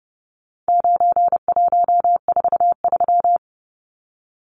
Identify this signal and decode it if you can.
Morse code, heard as a keyed tone, about 30 wpm, 700 Hz. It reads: 9143
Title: Numbers - 4-Digit 30wpm